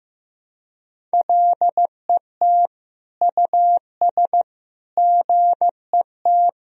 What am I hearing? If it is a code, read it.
Morse code, heard as a keyed tone, about 15 wpm, 700 Hz. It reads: LET US GET
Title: Sentences from Top 1000 Words - Review Part 4 - 15wpm